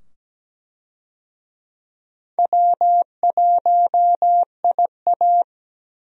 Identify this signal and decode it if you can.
Morse code, heard as a keyed tone, about 17 wpm, 700 Hz. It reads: W1IA
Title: USA Call Signs - 4 Characters - 17wpm